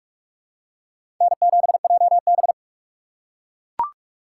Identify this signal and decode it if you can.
Morse code, heard as a keyed tone, about 43 wpm, 700 Hz. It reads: N7JB
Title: USA Call Signs - 4x Character Spacing - 45wpm